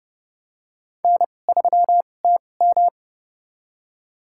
Morse code, heard as a keyed tone, about 31 wpm, spaced as for 22 wpm, 700 Hz. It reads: N3TM
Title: USA Call Signs - 4 Characters - 2x Character Spacing - 30wpm